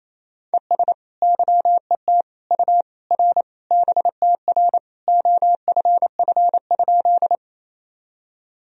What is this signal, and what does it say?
Morse code, heard as a keyed tone, about 28 wpm, 700 Hz. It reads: ES YET U R BTR OFF?
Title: Sentences from Top 500 Words and Abbreviations 28wpm